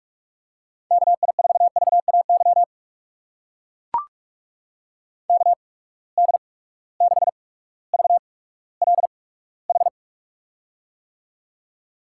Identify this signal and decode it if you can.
Morse code, heard as a keyed tone, about 43 wpm, spaced as for 19 wpm, 700 Hz. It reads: KI4VAY KD6VLH
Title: USA Call Signs - 6 Characters - 8x Character Spacing - 45wpm